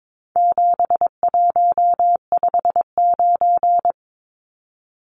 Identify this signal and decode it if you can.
Morse code, heard as a keyed tone, about 22 wpm, 700 Hz. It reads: 7159
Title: Numbers - 4-Digit 22wpm